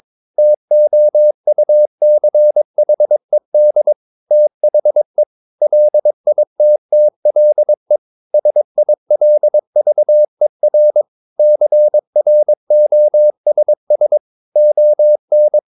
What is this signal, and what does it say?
Morse code, heard as a keyed tone, about 22 wpm, 600 Hz. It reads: TOUCHED THE LITTLE SILVER CROSS ON
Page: Romanen Never af Ken Follet på engelsk.